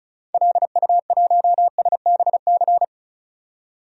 Morse code, heard as a keyed tone, about 34 wpm, 700 Hz. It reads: LU1SBC
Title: DXCC-100 Call Signs - 6 Characters - 35wpm